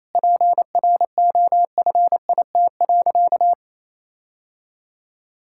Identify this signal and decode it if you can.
Morse code, heard as a keyed tone, about 27 wpm, 700 Hz. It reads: PROFIT.
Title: Sentences from Top 1600 Words 28wpm